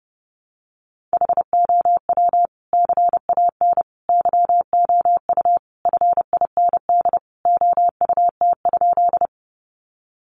Morse code, heard as a keyed tone, about 30 wpm, 700 Hz. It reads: HOW CAN YOU FIND OUT?
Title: Sentences from Top 200 Words 30wpm